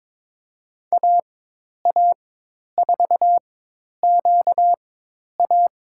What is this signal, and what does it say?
Morse code, heard as a keyed tone, about 22 wpm, spaced as for 10 wpm, 700 Hz. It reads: AA4QA
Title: USA Call Signs - 5 Characters - 4x Character Spacing - 22wpm